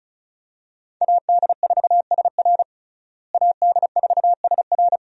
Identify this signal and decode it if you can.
Morse code, heard as a keyed tone, about 36 wpm, 700 Hz. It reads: AD4SR AD4SR
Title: USA Call Signs - 5 Characters - Repeated 2x - 35wpm